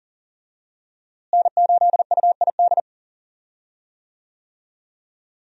Morse code, heard as a keyed tone, about 40 wpm, 700 Hz. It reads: N8UID
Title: USA Call Signs - 5 Characters - Repeated 2x - 40wpm